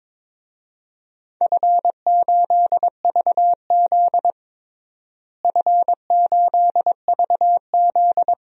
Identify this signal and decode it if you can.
Morse code, heard as a keyed tone, about 22 wpm, 700 Hz. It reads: F8VZ F8VZ
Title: DXCC-100 Call Signs - 4 Characters - Repeated 3x - 22wpm